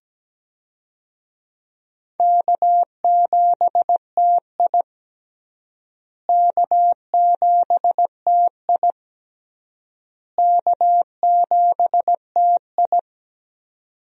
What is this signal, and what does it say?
Morse code, heard as a keyed tone, about 17 wpm, 700 Hz. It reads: K7TI K7TI K7TI
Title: USA Call Signs - 4 Characters - Repeated 3x - 17wpm